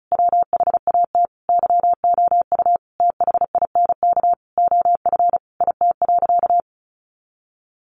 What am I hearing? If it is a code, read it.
Morse code, heard as a keyed tone, about 34 wpm, 700 Hz. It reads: WHAT YOU THINK OF IT.
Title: Sentences from Top 200 Words 35wpm